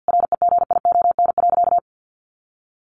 Morse code, heard as a keyed tone, about 50 wpm, 700 Hz. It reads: REGION.
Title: Sentences from Top 1500 Words 50wpm